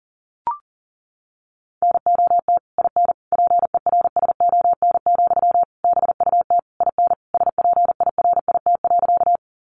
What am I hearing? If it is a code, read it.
Morse code, heard as a keyed tone, about 40 wpm, 700 Hz. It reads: NOT IN PERSON, BUT IN SPIRIT.
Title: Sentences from Top 1400 Words 40wpm